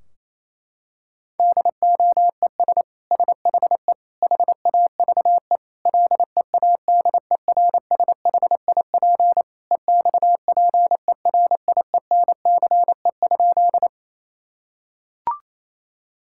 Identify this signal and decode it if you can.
Morse code, heard as a keyed tone, about 28 wpm, 700 Hz. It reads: DOES SHE HAVE LEADERSHIP EXPERIENCE?
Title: Sentences from Top 1600 Words 28wpm